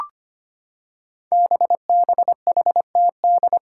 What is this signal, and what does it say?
Morse code, heard as a keyed tone, about 25 wpm, 700 Hz. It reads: BBHTD
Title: Five Challenging Letter-Number Combinations 25wpm